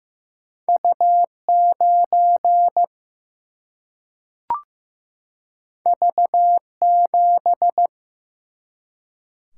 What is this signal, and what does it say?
Morse code, heard as a keyed tone, about 15 wpm, 700 Hz. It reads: U 9 V 7
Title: Single-Letter Single-Number 15wpm